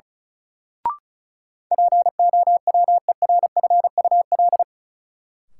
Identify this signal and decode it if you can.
Morse code, heard as a keyed tone, about 36 wpm, 700 Hz. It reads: POWERFUL
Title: Sentences from Top 1300 Words - New Word Drill Alt - 35wpm